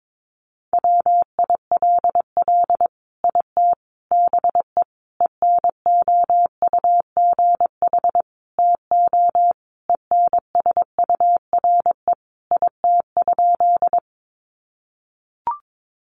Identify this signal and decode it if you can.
Morse code, heard as a keyed tone, about 22 wpm, 700 Hz. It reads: WILL IT BE ENOUGH TO ENSURE IT?
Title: Sentences from Top 1100 Words 22wpm